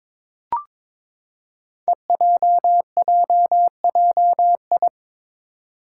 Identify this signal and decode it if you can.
Morse code, heard as a keyed tone, about 22 wpm, 700 Hz. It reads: EJJJI
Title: Five Challenging Letter-Number Combinations 22wpm